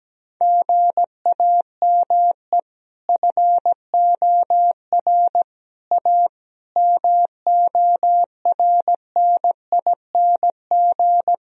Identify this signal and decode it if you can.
Morse code, heard as a keyed tone, about 17 wpm, 700 Hz. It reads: GAME FOR A MORNING
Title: Sentences from Top 1000 Words - Review Part 3 - 17wpm